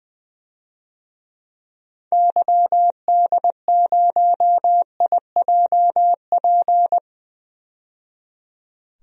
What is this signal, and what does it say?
Morse code, heard as a keyed tone, about 20 wpm, 700 Hz. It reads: YD0IJP
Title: DXCC-100 Call Signs - 6 Characters - 20wpm